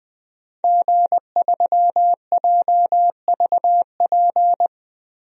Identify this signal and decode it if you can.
Morse code, heard as a keyed tone, about 20 wpm, 700 Hz. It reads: G3JVP
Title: DXCC-100 Call Signs - 5 Characters - Repeated 2x - 20wpm